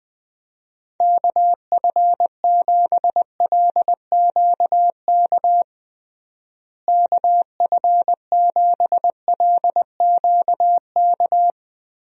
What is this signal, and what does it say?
Morse code, heard as a keyed tone, about 20 wpm, 700 Hz. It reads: KF7LQK KF7LQK
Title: USA Call Signs - 6 Characters - Repeated 3x - 20wpm